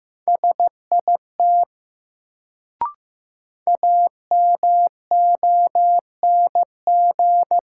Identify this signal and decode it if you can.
Morse code, heard as a keyed tone, about 15 wpm, 700 Hz. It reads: SIT AMONG
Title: Sentences from Top 500 Words - New Word Drill - 15wpm